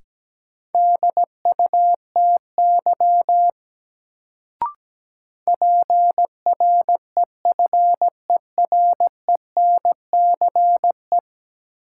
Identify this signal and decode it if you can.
Morse code, heard as a keyed tone, about 17 wpm, 700 Hz. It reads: DUTY PREFERENCE
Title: Sentences from Top 1500 Words - New Word Drill - 17wpm